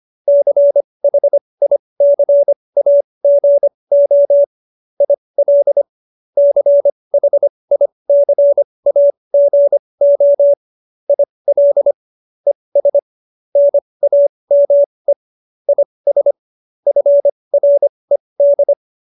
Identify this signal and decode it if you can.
Morse code, heard as a keyed tone, about 25 wpm, spaced as for 20 wpm, 570 Hz. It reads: CHICAGO IL CHICAGO IL ES NAME IS FRED